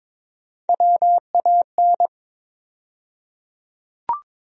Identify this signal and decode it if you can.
Morse code, heard as a keyed tone, about 22 wpm, 700 Hz. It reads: WAN
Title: USA Call Sign Postfixes - 3 Letters 22wpm